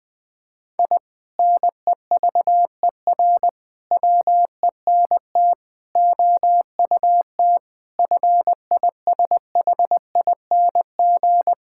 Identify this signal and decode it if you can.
Morse code, heard as a keyed tone, about 20 wpm, 700 Hz. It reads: I NEVER WENT OUT FISHING
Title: Sentences from Top 1000 Words - Review Part 5 - 20wpm